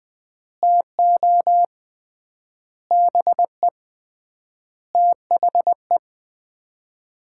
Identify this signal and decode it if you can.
Morse code, heard as a keyed tone, about 20 wpm, 700 Hz. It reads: TO BE THE
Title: 3x Word-Spacing - Sets of 5 Words - 20wpm